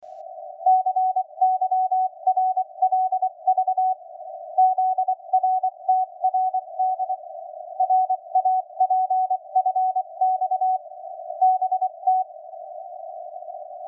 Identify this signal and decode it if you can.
Morse code, heard as a keyed tone, about 24 wpm, 750 Hz. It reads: CYRLV ZRTRD RAPFX BT